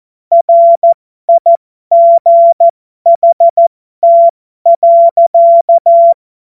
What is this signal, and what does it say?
Morse code, heard as a keyed tone, about 14 wpm, spaced as for 12 wpm, 680 Hz. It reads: RIGHT.
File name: Morsekurs / Witze